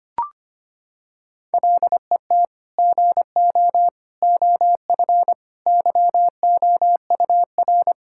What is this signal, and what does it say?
Morse code, heard as a keyed tone, about 25 wpm, 700 Hz. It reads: LET GO OF YOUR
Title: Sentences from Top 1600 Words 25wpm